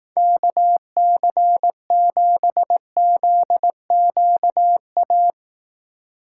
Morse code, heard as a keyed tone, about 18 wpm, 700 Hz. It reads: KC7ZQA
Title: Speed Racing - USA 2x3 Call Signs 15wpm